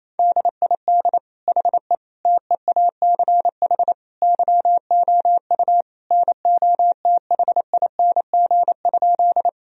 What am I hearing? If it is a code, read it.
Morse code, heard as a keyed tone, about 28 wpm, 700 Hz. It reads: DID HE TEACH YOU NOTHING?
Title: Sentences from Top 1000 Words - Review Part 2 - 28wpm